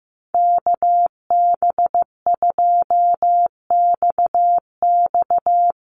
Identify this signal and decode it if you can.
Morse code, heard as a keyed tone, about 15 wpm, 700 Hz. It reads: KB2XX
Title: NA SOTA Chaser Callsigns 15wpm